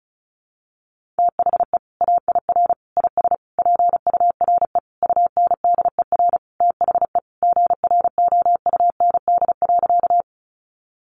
Morse code, heard as a keyed tone, about 34 wpm, 700 Hz. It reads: THE AIR IS PURE UNDER THE GROUND.
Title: Sentences from Top 1500 Words 35wpm